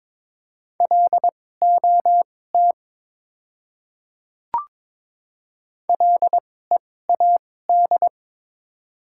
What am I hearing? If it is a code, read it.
Morse code, heard as a keyed tone, about 22 wpm, spaced as for 16 wpm, 690 Hz. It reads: LOT LEAD
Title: 2x Character Spacing - Top 500 Words - 22wpm